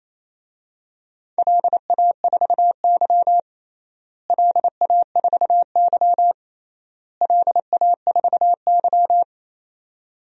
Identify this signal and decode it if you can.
Morse code, heard as a keyed tone, about 27 wpm, 700 Hz. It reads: LA4Y LA4Y LA4Y
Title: DXCC-100 Call Signs - 4 Characters - Repeated 3x - 28wpm